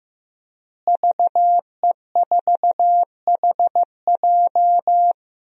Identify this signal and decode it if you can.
Morse code, heard as a keyed tone, about 15 wpm, 700 Hz. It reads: VE4HJ
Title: DXCC-100 Call Signs - 5 Characters - Repeated 2x - 15wpm